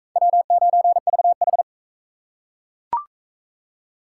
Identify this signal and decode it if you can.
Morse code, heard as a keyed tone, about 37 wpm, 700 Hz. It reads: W9VH